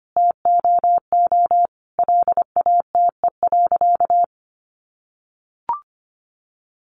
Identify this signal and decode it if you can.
Morse code, heard as a keyed tone, about 25 wpm, 700 Hz. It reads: TOO LATE.
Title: Sentences from Top 300 Words 25wpm